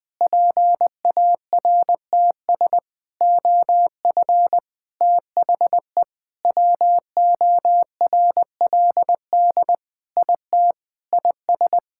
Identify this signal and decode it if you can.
Morse code, heard as a keyed tone, about 21 wpm, 700 Hz. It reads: PARTS OF THE WORLD IT IS
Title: Sentences with 6-Word Phrases - Spoken Beginning - 20wpm